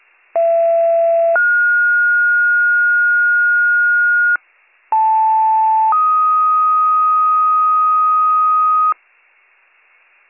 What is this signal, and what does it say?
Two-tone page, heard as a one-second tone, then a three-second tone, about 670 then 1430 Hz, about 880 then 1250 Hz. It is heard from 0.3 to 8.9 s.